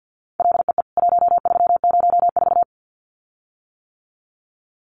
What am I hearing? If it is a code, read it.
Morse code, heard as a keyed tone, about 48 wpm, 700 Hz. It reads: LEE 1304
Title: CWT Exchange 50wpm